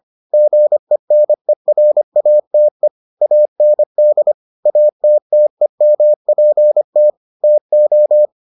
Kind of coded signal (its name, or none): Morse code